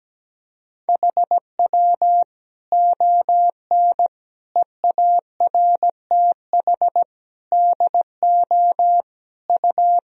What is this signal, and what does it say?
Morse code, heard as a keyed tone, about 17 wpm, 700 Hz. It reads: HW ON EARTH DO U
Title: Sentences from Top 200 Words and Abbreviations 17wpm